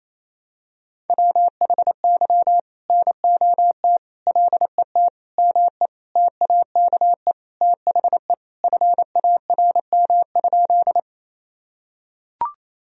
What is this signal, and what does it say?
Morse code, heard as a keyed tone, about 28 wpm, 700 Hz. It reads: WHY NOT LET ME TAKE THE FARM?
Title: Sentences from Top 1000 Words - Review Part 2 - 28wpm